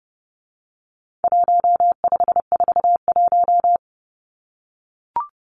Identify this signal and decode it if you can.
Morse code, heard as a keyed tone, about 28 wpm, 700 Hz. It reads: 1541
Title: Numbers - 4-Digit 30wpm